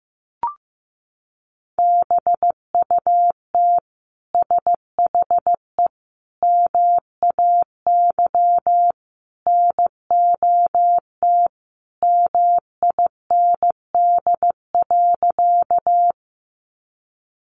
Morse code, heard as a keyed tone, about 14 wpm, 700 Hz. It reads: BUT SHE MAY NOT MIND.
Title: Sentences from Top 600 Words 15wpm